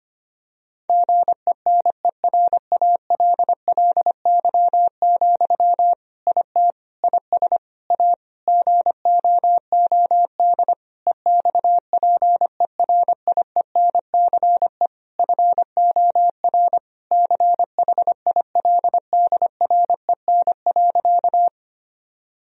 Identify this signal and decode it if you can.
Morse code, heard as a keyed tone, about 25 wpm, 700 Hz. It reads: GENERALLY, IT IS A GOOD EXPERIENCE FOR CHILDREN.
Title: Sentences from Top 1000 Words - Review Part 4 - 25wpm